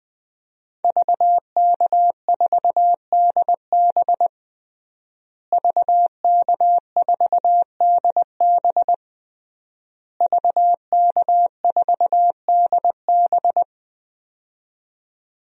Morse code, heard as a keyed tone, about 20 wpm, 700 Hz. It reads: VK4DB VK4DB VK4DB
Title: DXCC-100 Call Signs - 5 Characters - Repeated 3x - 20wpm